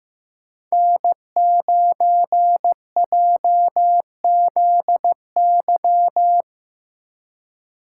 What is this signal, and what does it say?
Morse code, heard as a keyed tone, about 15 wpm, 700 Hz. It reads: N9JZY
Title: K1USN SST QSO Practice - 15wpm